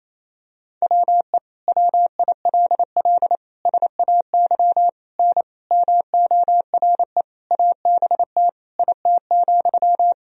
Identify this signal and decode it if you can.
Morse code, heard as a keyed tone, about 28 wpm, 700 Hz. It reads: WE WILL SAY N MORE ABT IT,
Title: Sentences from Top 200 Words and Abbreviations 28wpm